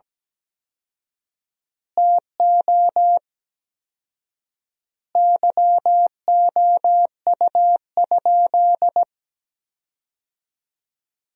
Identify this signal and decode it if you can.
Morse code, heard as a keyed tone, about 17 wpm, 700 Hz. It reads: TO YOU?
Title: 4x Word-Spacing - Sentences from Top 100 Words - 17wpm